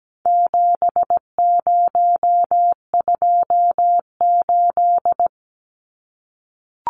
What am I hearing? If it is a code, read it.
Morse code, heard as a keyed tone, about 17 wpm, 700 Hz. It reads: 7028
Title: Numbers - 4-Digit 17wpm